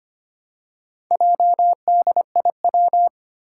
Morse code, heard as a keyed tone, about 25 wpm, 700 Hz. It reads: JDIW
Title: Four Challenging Letter-Number Combinations 25wpm